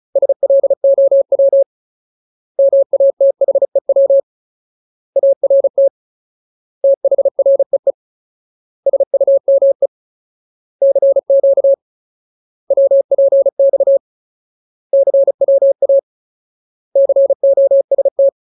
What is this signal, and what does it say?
Morse code, heard as a keyed tone, about 37 wpm, 550 Hz. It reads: SLOW MATHEW ART THREE SUME CQ WPX CWA COST
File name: Ultimate Word List / 35WPM Ultimate